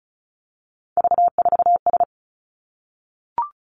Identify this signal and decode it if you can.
Morse code, heard as a keyed tone, about 36 wpm, 700 Hz. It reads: V4S
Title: Three Challenging Letter-Number Combinations 35wpm